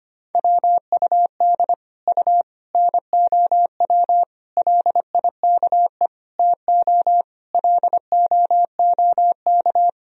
Morse code, heard as a keyed tone, about 25 wpm, 700 Hz. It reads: WUD U NOW LIKE TO LOOK
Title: Sentences from Top 200 Words and Abbreviations 25wpm